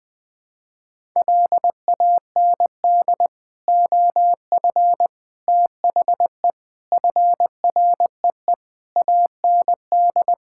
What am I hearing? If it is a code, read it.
Morse code, heard as a keyed tone, about 20 wpm, 700 Hz. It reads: LAND OF THE FREE AND
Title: Sentences with 9-Word Phrases - 20wpm